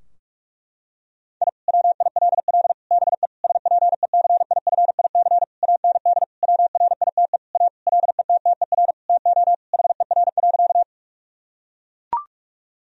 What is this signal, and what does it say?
Morse code, heard as a keyed tone, about 43 wpm, 700 Hz. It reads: I WILL BE SPECIFIC AND WRITE A LETTER TO HER.
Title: Sentences from Top 1000 Words - Review Part 3 - 45wpm